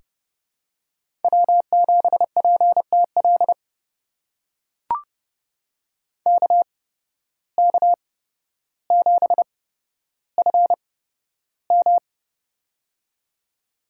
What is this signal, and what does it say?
Morse code, heard as a keyed tone, about 29 wpm, spaced as for 13 wpm, 700 Hz. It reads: W7PTL KK7FM
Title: USA Call Signs - 5 Characters - 8x Character Spacing - 30wpm